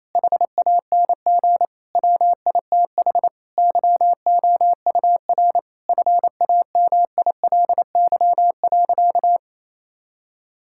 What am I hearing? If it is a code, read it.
Morse code, heard as a keyed tone, about 28 wpm, 700 Hz. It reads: HANG WITH YOUR FAMILY.
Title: Sentences from Top 1600 Words 28wpm